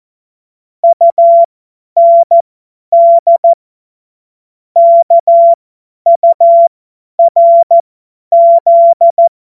Morse code, heard as a keyed tone, about 14 wpm, spaced as for 10 wpm, 680 Hz. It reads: UND KURZ
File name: Morsekurs / Witze / Witz 16. Dezember 2025 50 F70